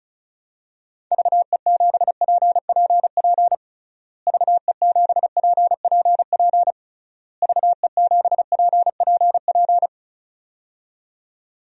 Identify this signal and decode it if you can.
Morse code, heard as a keyed tone, about 36 wpm, 700 Hz. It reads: VE7PPP VE7PPP VE7PPP
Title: DXCC-100 Call Signs - 6 Characters - Repeated 3x - 35wpm